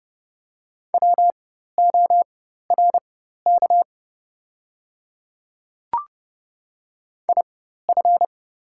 Morse code, heard as a keyed tone, about 30 wpm, spaced as for 14 wpm, 700 Hz. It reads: WORK IF
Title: 4x Character Spacing - Top 200 Words - 30wpm